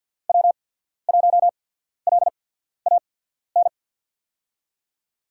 Morse code, heard as a keyed tone, about 53 wpm, spaced as for 14 wpm, 700 Hz. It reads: W1LAN
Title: USA Call Signs - 5 Characters - 8x Character Spacing - 50wpm